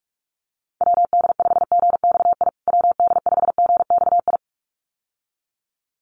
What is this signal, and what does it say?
Morse code, heard as a keyed tone, about 44 wpm, 700 Hz. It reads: WD5GXI WD5GXI
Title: USA Call Signs Repeated 2x 45wpm